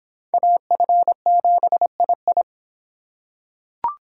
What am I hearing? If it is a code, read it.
Morse code, heard as a keyed tone, about 22 wpm, 700 Hz. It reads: AF7II